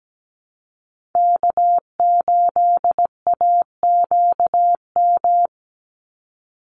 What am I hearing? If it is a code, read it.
Morse code, heard as a keyed tone, about 17 wpm, 700 Hz. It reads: K8AQM
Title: Call Signs Repeated 3x - 17wpm